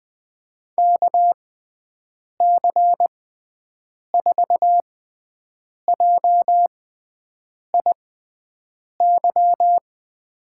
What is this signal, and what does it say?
Morse code, heard as a keyed tone, about 20 wpm, spaced as for 7 wpm, 700 Hz. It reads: KC4JIY
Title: USA Call Signs - 6 Characters - 6x Character Spacing - 20wpm